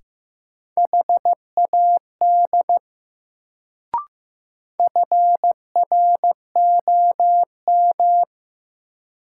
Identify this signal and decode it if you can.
Morse code, heard as a keyed tone, about 15 wpm, 700 Hz. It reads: HAD FROM
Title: Sentences from Top 100 Words - New Word Drill - 15wpm